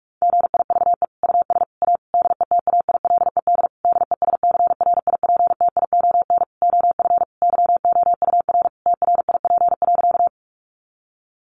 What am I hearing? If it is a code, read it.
Morse code, heard as a keyed tone, about 44 wpm, 700 Hz. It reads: GIVE US A DETAILED DESCRIPTION OF YOUR TRIP.
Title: Sentences from Top 1500 Words 45wpm